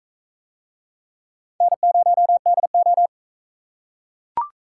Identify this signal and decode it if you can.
Morse code, heard as a keyed tone, about 44 wpm, 700 Hz. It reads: N0DO